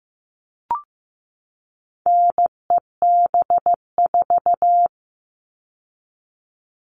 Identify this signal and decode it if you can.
Morse code, heard as a keyed tone, about 15 wpm, 700 Hz. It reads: NEB4
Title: All Cut Numbers - 4-Digit 15wpm